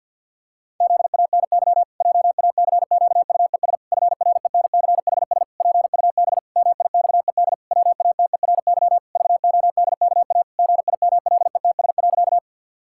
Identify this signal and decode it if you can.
Morse code, heard as a keyed tone, about 51 wpm, 700 Hz. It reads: ZANY JACQUES FRENCHS PUB MIXED WATERY VODKA GIMLETS.
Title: Pangram Challenge v2 - 50wpm